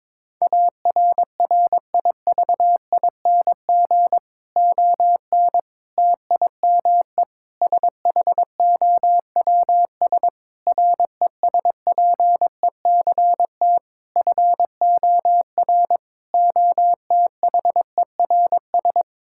Morse code, heard as a keyed tone, about 23 wpm, 700 Hz. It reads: ARRIVING ON TIME SHOWS RESPECT FOR OTHERS
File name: Difficult Sentences with Hints - 22wpm